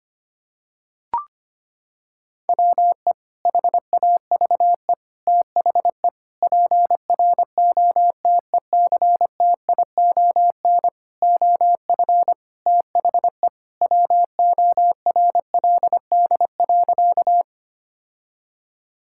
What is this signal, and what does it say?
Morse code, heard as a keyed tone, about 25 wpm, 700 Hz. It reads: WE HAVE THE PROTECTION OF THE WORLD.
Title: Sentences from Top 1600 Words 25wpm